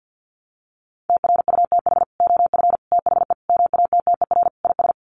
Morse code, heard as a keyed tone, about 50 wpm, 700 Hz. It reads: TRUTH OF THE MATTER IS
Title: Sets of 5 Words Encore2 50wpm